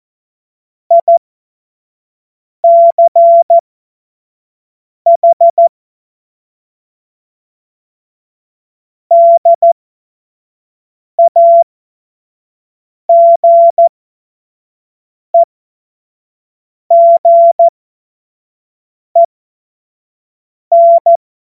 Morse code, heard as a keyed tone, about 14 wpm, spaced as for 5 wpm, 680 Hz. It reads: ICH DAGEGEN
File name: Morsekurs / Witze / Witz 23 April 2024 25 F70